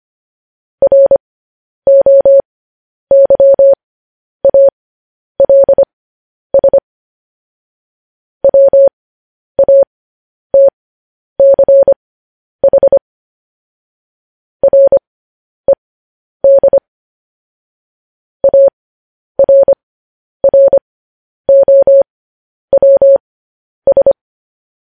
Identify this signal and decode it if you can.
Morse code, heard as a keyed tone, about 25 wpm, spaced as for 10 wpm, 550 Hz. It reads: ROYALS WATCH RED ARROWS
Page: News Headlines in Morse Code at 10 WPM